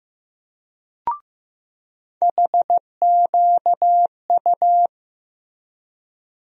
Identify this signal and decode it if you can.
Morse code, heard as a keyed tone, about 15 wpm, 700 Hz. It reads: HQU
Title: USA Call Sign Postfixes - 3 Letters 15wpm